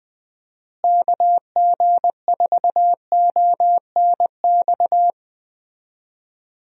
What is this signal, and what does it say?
Morse code, heard as a keyed tone, about 20 wpm, 700 Hz. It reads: KG4ONX
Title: USA Call Signs - 6 Characters - Repeated 2x - 20wpm